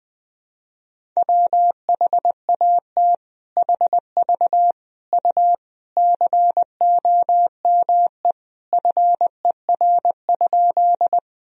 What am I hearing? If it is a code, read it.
Morse code, heard as a keyed tone, about 20 wpm, 700 Hz. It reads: WHAT HV U COME FER?
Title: Sentences from Top 100 Words and Abbreviations 20wpm